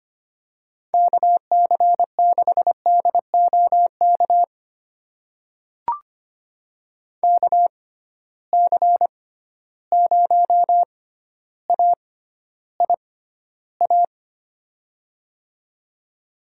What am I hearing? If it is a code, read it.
Morse code, heard as a keyed tone, about 25 wpm, spaced as for 13 wpm, 700 Hz. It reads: KC6DOK KC0AIA
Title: USA Call Signs - 6 Characters - 6x Character Spacing - 25wpm